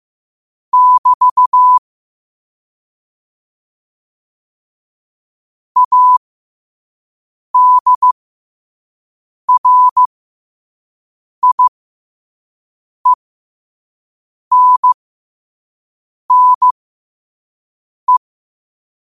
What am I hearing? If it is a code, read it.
Morse code, heard as a keyed tone, about 15 wpm, spaced as for 5 wpm, 1000 Hz. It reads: <BT> ADRIENNE
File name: qotdfiles / 5wpm